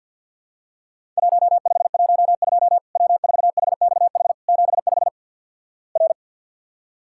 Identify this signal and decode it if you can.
Morse code, heard as a keyed tone, about 50 wpm, 700 Hz. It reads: 1512 W4HXH 75 R
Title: Worked All Europe DX Contest - QTC Exchange - 50wpm